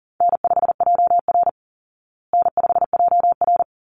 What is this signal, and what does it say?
Morse code, heard as a keyed tone, about 40 wpm, 700 Hz. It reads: N5JR N5JR
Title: Call Signs Repeated 3x - 40wpm